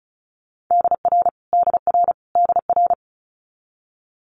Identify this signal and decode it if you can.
Morse code, heard as a keyed tone, about 36 wpm, 700 Hz. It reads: DR DR DR
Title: Multi-letter Phonemes Repeated 3x 35wpm